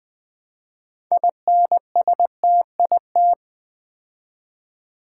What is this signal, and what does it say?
Morse code, heard as a keyed tone, about 20 wpm, 700 Hz. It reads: INSTIT
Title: Build A Word - 3 Letters At A Time - 9-Letter Words - 20wpm